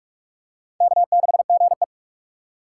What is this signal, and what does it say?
Morse code, heard as a keyed tone, about 41 wpm, 700 Hz. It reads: K6GE
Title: USA Call Signs - 4 Characters - 4x Character Spacing - 45wpm